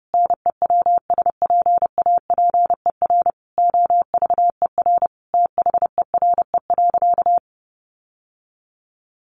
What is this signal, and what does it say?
Morse code, heard as a keyed tone, about 30 wpm, 700 Hz. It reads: NEWSPAPER OVER THERE.
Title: Sentences from Top 1100 Words 30wpm